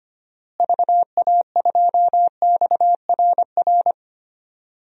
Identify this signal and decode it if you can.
Morse code, heard as a keyed tone, about 25 wpm, 700 Hz. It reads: VA2XRR
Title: DXCC-100 Call Signs - 6 Characters - Repeated 3x - 25wpm